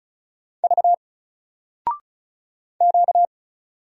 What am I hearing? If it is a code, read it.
Morse code, heard as a keyed tone, about 36 wpm, 700 Hz. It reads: V Q
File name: Single-Letter-Number-35wpm